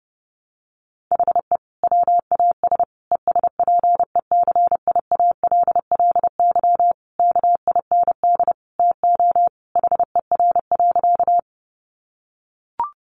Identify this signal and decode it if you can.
Morse code, heard as a keyed tone, about 30 wpm, 700 Hz. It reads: HE WAS ESPECIALLY KIND TO HER.
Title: Sentences from Top 700 Words 30wpm